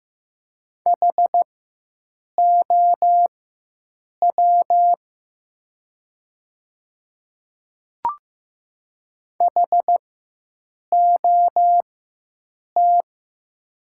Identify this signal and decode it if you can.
Morse code, heard as a keyed tone, about 15 wpm, spaced as for 7 wpm, 700 Hz. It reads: HOW HOT
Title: 4x Character Spacing - Top 200 Words - 15wpm